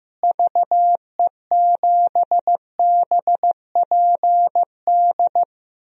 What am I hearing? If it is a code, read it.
Morse code, heard as a keyed tone, about 16 wpm, 700 Hz. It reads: VE7BPD
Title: DXCC-100 Call Signs - 6 Characters - Repeated 3x - 15wpm